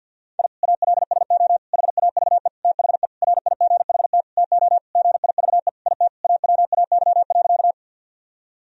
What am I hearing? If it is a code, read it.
Morse code, heard as a keyed tone, about 51 wpm, 700 Hz. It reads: I ALSO HAVE THE RIGHT TO GIVE IT AWAY.
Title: Sentences from Top 1000 Words - Review Part 1 - 50wpm